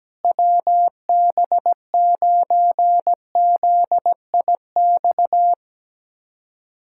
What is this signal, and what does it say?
Morse code, heard as a keyed tone, about 17 wpm, 700 Hz. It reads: WB9ZIX
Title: USA Callsign DE Callsign - 6 Characters - 17wpm